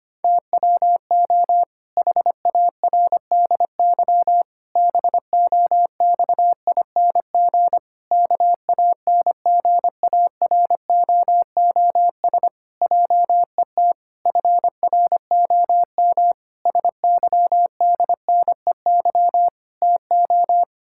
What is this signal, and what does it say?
Morse code, heard as a keyed tone, about 25 wpm, 700 Hz. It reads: TWO HARDY BOXING KANGAROOS JET FROM SYDNEY TO
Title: Pangram Challenge v2 - 25wpm